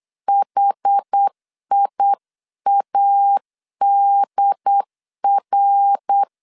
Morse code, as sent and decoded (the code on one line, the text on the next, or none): .... .. .- -.. .-.
HIADR